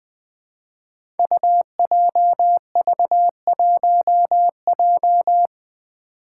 Morse code, heard as a keyed tone, about 20 wpm, 700 Hz. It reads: UJV1J
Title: Five Challenging Letter-Number Combinations 20wpm